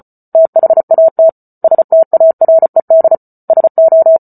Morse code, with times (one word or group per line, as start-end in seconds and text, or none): 0.3-1.3 THAT
1.6-3.1 STARED
3.5-4.2 SO